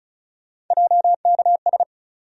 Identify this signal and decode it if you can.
Morse code, heard as a keyed tone, about 36 wpm, 700 Hz. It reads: JKS
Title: USA Call Sign Postfixes - 3 Letters 35wpm